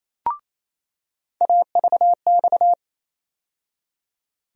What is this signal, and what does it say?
Morse code, heard as a keyed tone, about 28 wpm, 700 Hz. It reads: AVX
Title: USA Call Sign Postfixes - 3 Letters 28wpm